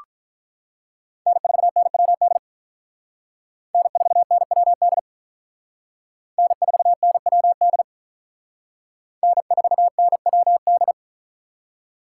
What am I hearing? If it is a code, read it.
Morse code, heard as a keyed tone, about 45 wpm, 700 Hz. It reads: N4NWD N4NWD N4NWD N4NWD